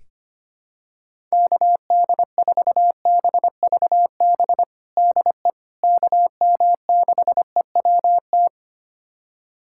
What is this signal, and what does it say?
Morse code, heard as a keyed tone, about 25 wpm, 700 Hz. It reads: KD4BVB DE KM6EWT
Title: USA Callsign DE Callsign - 6 Characters - 25wpm